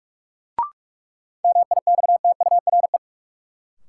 Morse code, heard as a keyed tone, about 45 wpm, 700 Hz. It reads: MIXTURE
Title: Sentences from Top 1500 Words - New Word Drill Alt - 45wpm